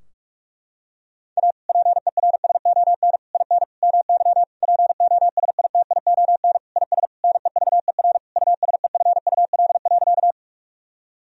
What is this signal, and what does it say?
Morse code, heard as a keyed tone, about 46 wpm, 700 Hz. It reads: A PERSON IN MY POSITION IS NEVER USEFUL.
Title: Sentences from Top 1000 Words - Review Part 4 - 45wpm